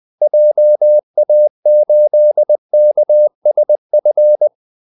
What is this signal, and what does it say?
Morse code, heard as a keyed tone, about 20 wpm, 600 Hz. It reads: JA8KSF
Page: Kaldesignaler 20 wpm